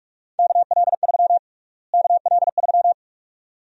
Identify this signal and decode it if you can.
Morse code, heard as a keyed tone, about 42 wpm, 700 Hz. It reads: KL3 KL3
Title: USA Call Sign Prefixes - Repeated 2x - 45wpm